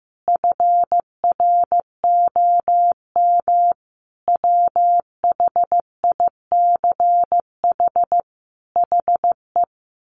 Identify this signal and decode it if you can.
Morse code, heard as a keyed tone, about 15 wpm, 700 Hz. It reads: FROM WHICH HE
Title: Sentences from Top 1500 Words 15wpm